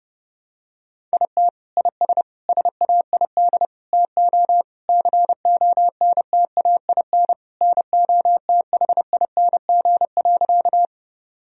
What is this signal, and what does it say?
Morse code, heard as a keyed tone, about 30 wpm, 700 Hz. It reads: IT IS SAID TO CONTAIN NOTHING.
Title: Sentences from Top 1000 Words - Review Part 2 - 30wpm